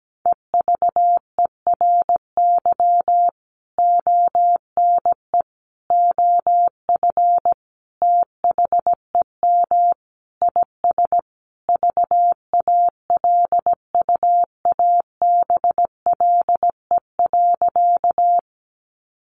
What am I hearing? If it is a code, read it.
Morse code, heard as a keyed tone, about 17 wpm, 700 Hz. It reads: EVERY ONE OF THEM IS VALUABLE.
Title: Sentences from Top 1100 Words 17wpm